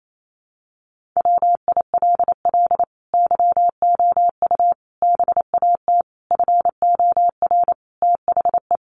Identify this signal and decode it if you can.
Morse code, heard as a keyed tone, about 28 wpm, 700 Hz. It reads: WILL YOU BAT FOR THE
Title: Sentences from Top 1500 Words 28wpm